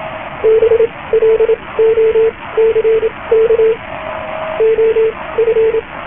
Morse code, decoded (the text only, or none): BLOCK OF